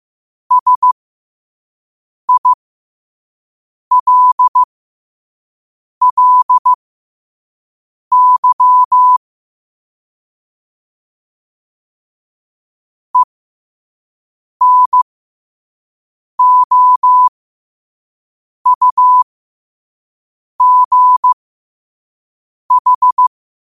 Morse code, heard as a keyed tone, about 15 wpm, spaced as for 5 wpm, 1000 Hz. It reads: SILLY ENOUGH
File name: qotdfiles / 5wpm